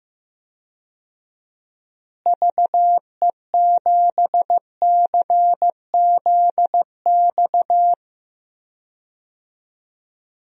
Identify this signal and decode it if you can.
Morse code, heard as a keyed tone, about 15 wpm, 700 Hz. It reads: VE7CZX
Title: DXCC-100 Call Signs - 6 Characters - Repeated 2x - 15wpm